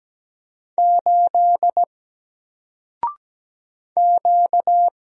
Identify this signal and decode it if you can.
Morse code, heard as a keyed tone, about 17 wpm, 700 Hz. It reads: 8 Q
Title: Single Letter-Number Alt 17wpm